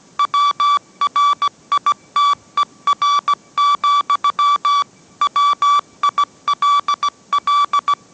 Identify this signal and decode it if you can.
Morse code, heard as a keyed tone, about 17 wpm, 1150 Hz. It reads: WRITER, WILL